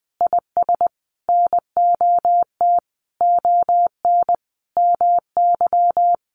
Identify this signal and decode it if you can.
Morse code, heard as a keyed tone, about 20 wpm, 700 Hz. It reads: IS NOT ON MY
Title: Sentences from Top 1400 Words 20wpm